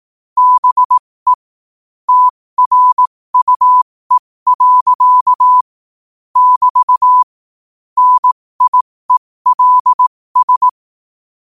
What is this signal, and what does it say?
Morse code, heard as a keyed tone, about 18 wpm, spaced as for 15 wpm, 1000 Hz. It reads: BE TRUE. <BT> NIELS
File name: qotdfiles / 15wpm